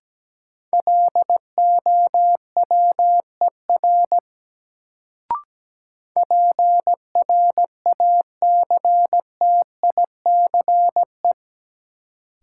Morse code, heard as a keyed tone, about 17 wpm, 700 Hz. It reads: LOWER PRACTICE
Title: Sentences from Top 700 Words - New Word Drill - 17wpm